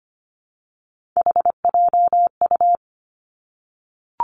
Morse code, heard as a keyed tone, about 25 wpm, 700 Hz. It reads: HJU
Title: Three Challenging Letter-Number Combinations 25wpm